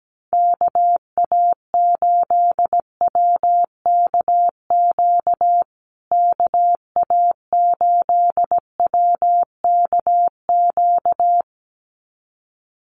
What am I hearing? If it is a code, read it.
Morse code, heard as a keyed tone, about 17 wpm, 700 Hz. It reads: KA8WKQ KA8WKQ
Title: USA Call Signs Repeated 2x 17wpm